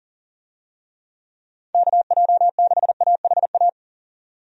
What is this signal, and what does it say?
Morse code, heard as a keyed tone, about 40 wpm, 700 Hz. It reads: KJ6AHA
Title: USA Call Signs - 6 Characters - 40wpm